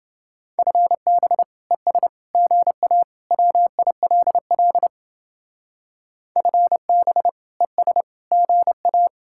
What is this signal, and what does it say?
Morse code, heard as a keyed tone, about 32 wpm, 700 Hz. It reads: FB ES GA WILL FB ES GA